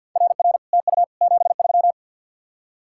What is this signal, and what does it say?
Morse code, heard as a keyed tone, about 50 wpm, 700 Hz. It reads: RR TU 73
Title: Speed-Racing Common QSO Phrases - 50wpm